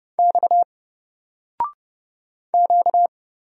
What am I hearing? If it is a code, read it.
Morse code, heard as a keyed tone, about 30 wpm, 700 Hz. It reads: X Q
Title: Single Letter-Number 30wpm